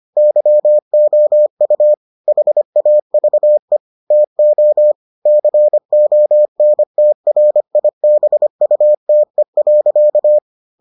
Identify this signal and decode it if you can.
Morse code, heard as a keyed tone, about 25 wpm, 600 Hz. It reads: YOU HAVE TO CONTRIBUTE.